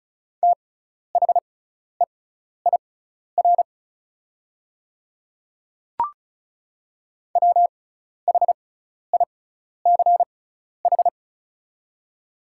Morse code, heard as a keyed tone, about 36 wpm, spaced as for 12 wpm, 700 Hz. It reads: THEIR WHICH
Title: 6x Character Spacing - Top 300 Words - 35wpm